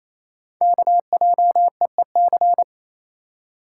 Morse code, heard as a keyed tone, about 29 wpm, 700 Hz. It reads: KJEEC
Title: Code Groups - 5 Letters - 28wpm